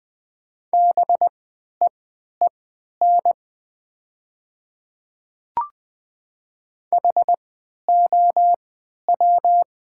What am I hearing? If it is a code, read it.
Morse code, heard as a keyed tone, about 20 wpm, spaced as for 11 wpm, 700 Hz. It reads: BEEN HOW
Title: 3x Character Spacing - Top 100 Words - 20wpm